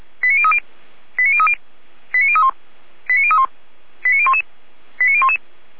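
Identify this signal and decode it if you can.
Five-tone selective call (ZVEI), heard as quick runs of five tones, tones of about 70 ms. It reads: 89030 89030 89031 89031 89011 89011
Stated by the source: Five-tone ZVEI selective call 89001 received by SDR, from feuerwehr-orlamuende.org fire brigade server.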